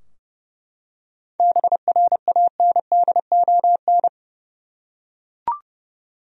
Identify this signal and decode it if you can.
Morse code, heard as a keyed tone, about 32 wpm, 700 Hz. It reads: BRANDON
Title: USA Common Names - 30wpm